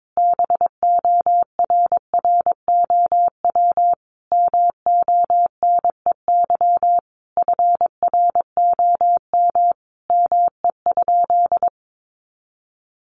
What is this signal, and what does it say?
Morse code, heard as a keyed tone, about 22 wpm, 700 Hz. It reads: BORROW MONEY FROM ME?
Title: Sentences from Top 1500 Words 22wpm